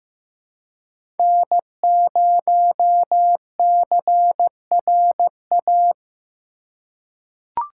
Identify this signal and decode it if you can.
Morse code, heard as a keyed tone, about 16 wpm, 700 Hz. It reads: N0CRA
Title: USA Call Signs - 3x Character Spacing - 15wpm